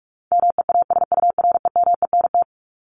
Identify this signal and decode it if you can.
Morse code, heard as a keyed tone, about 44 wpm, 700 Hz. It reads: MEASUREMENT
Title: Top 1751-2000 Words 45wpm